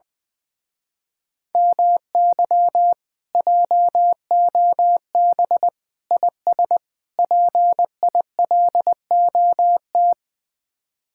Morse code, heard as a keyed tone, about 20 wpm, 700 Hz. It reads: MY JOB IS PILOT
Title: Speed-Racing Common QSO Phrases - 20wpm